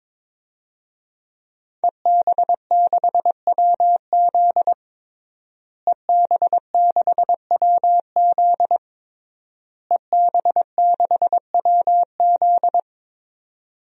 Morse code, heard as a keyed tone, about 22 wpm, 700 Hz. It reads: EB6WZ EB6WZ EB6WZ
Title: DXCC-100 Call Signs - 5 Characters - Repeated 3x - 22wpm